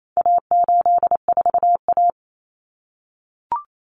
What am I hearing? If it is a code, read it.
Morse code, heard as a keyed tone, about 28 wpm, 700 Hz. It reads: A84A
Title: Common Cut Numbers - 4-Digit 28wpm